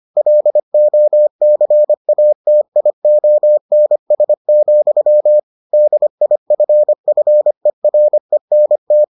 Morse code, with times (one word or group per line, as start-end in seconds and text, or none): 0.2-5.4 LOCATIONS,
5.7-9.0 DIFFERENT